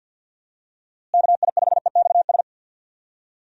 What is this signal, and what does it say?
Morse code, heard as a keyed tone, about 53 wpm, 700 Hz. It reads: KI5EXS
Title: USA Call Signs - 6 Characters - Repeated 3x - 50wpm